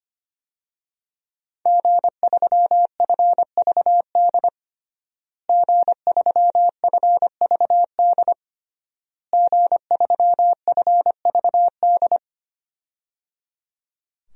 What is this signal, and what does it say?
Morse code, heard as a keyed tone, about 25 wpm, 700 Hz. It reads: G3FVD G3FVD G3FVD
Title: DXCC-100 Call Signs - 5 Characters - Repeated 3x - 25wpm